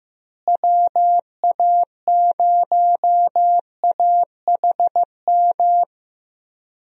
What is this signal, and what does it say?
Morse code, heard as a keyed tone, about 15 wpm, 700 Hz. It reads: WA0AHM
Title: USA Call Signs - 6 Characters - Repeated 3x - 15wpm